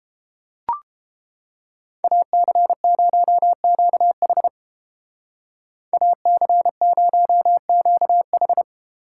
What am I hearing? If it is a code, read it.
Morse code, heard as a keyed tone, about 32 wpm, 700 Hz. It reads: AC0QH AC0QH